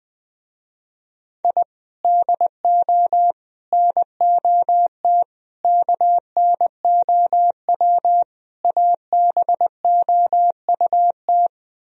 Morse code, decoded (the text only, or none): I DO NOT KNOW ABOUT